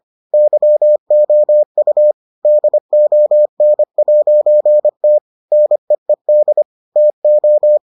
Morse code, heard as a keyed tone, about 25 wpm, 600 Hz. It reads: YOU DON'T NEED TO